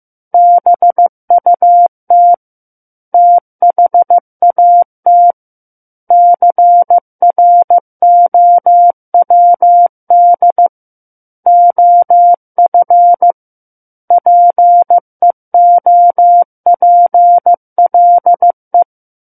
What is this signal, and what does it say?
Morse code, heard as a keyed tone, about 15 wpm, 700 Hz. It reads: BUT THAT CROWD OF PEOPLE
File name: Book One / 06-Chapter 6 - 15 WPM